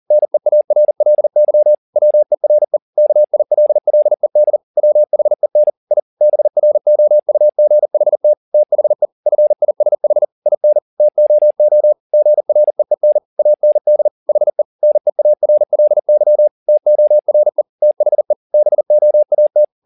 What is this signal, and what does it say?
Morse code, heard as a keyed tone, about 40 wpm, 600 Hz. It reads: NEARLY WERE KILLED WHEN I BROUGHT THE FISH IN TOO GREEN AND HE NEARLY TORE THE BOAT
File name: the old man and the sea 1 40wpm